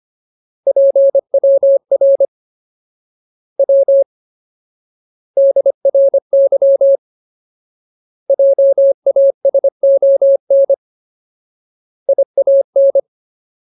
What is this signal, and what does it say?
Morse code, heard as a keyed tone, about 25 wpm, 550 Hz. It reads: PWR W DRY JASON IAN
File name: Ultimate Word List / 25WPM Ultimate